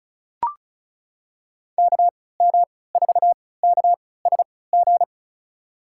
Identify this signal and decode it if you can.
Morse code, heard as a keyed tone, about 36 wpm, spaced as for 20 wpm, 700 Hz. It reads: KM4KSG
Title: USA Call Signs - 3x Character Spacing - 35wpm